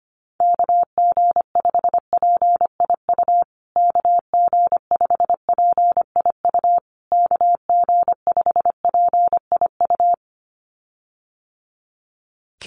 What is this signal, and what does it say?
Morse code, heard as a keyed tone, about 25 wpm, 700 Hz. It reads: KG5PIU KG5PIU KG5PIU
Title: USA Call Signs Repeated 3x 25wpm